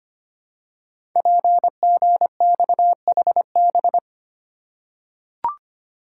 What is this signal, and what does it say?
Morse code, heard as a keyed tone, about 25 wpm, 700 Hz. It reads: PGXHB
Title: Code Groups - 5 Letters - 25wpm